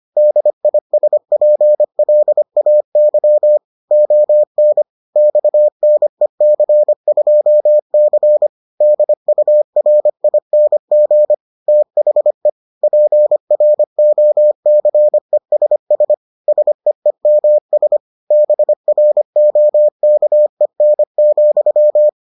Morse code, heard as a keyed tone, about 25 wpm, 600 Hz. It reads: DISPLAY ON XNEC2C DURING THE PROCESS SEEMS BROKEN,